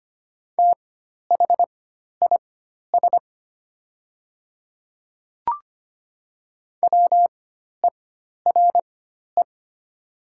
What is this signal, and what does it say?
Morse code, heard as a keyed tone, about 25 wpm, spaced as for 12 wpm, 700 Hz. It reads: THIS WERE
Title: 4x Character Spacing - Top 100 Words - 25wpm